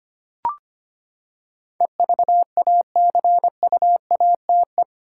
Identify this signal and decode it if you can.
Morse code, heard as a keyed tone, about 25 wpm, 700 Hz. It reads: EVACUATE
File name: Eight Letter Words - 25wpm